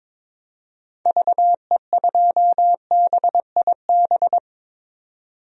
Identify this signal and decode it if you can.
Morse code, heard as a keyed tone, about 22 wpm, 700 Hz. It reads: VE2BIB
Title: DXCC-100 Call Signs - 6 Characters - Repeated 2x - 22wpm